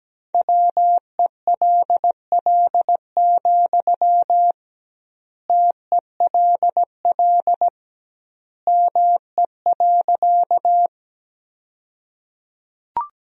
Morse code, heard as a keyed tone, about 17 wpm, 700 Hz. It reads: WELL, TELL ME.
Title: 2x Word-Spacing - Sentences from Top 200 Words - 17wpm